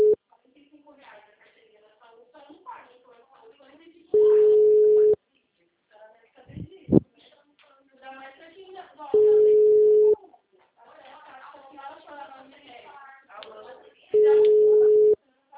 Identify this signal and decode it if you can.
Ringback tone (Europe, 425 Hz), with speech.